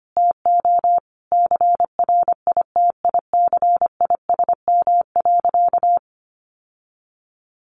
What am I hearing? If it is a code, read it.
Morse code, heard as a keyed tone, about 25 wpm, 700 Hz. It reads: TO CRITICISM.
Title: Sentences from Top 1300 Words 25wpm